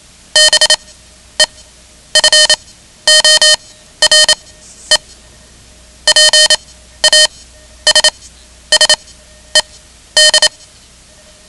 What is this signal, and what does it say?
Morse code, heard as a keyed tone, about 27 wpm, spaced as for 12 wpm, 3050 Hz. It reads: BEFORE PASSED
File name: ham radio / CodeQuick / Code Quick 10